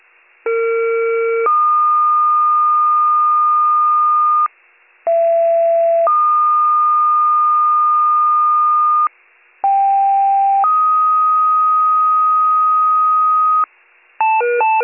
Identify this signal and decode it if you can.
Two-tone page, heard as a one-second tone, then a three-second tone, about 460 then 1220 Hz, about 670 then 1220 Hz, about 790 then 1250 Hz. It is heard from 0.4 to 13.6 s.